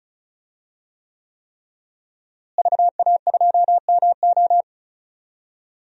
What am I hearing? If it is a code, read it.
Morse code, heard as a keyed tone, about 36 wpm, 690 Hz. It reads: VA2MO
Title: DXCC-100 Call Signs - 5 Characters - Repeated 3x - 35wpm